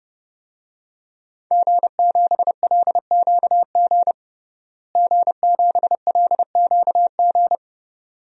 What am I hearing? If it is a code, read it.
Morse code, heard as a keyed tone, about 30 wpm, 700 Hz. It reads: G7LQG G7LQG
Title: DXCC-100 Call Signs - 5 Characters - Repeated 2x - 30wpm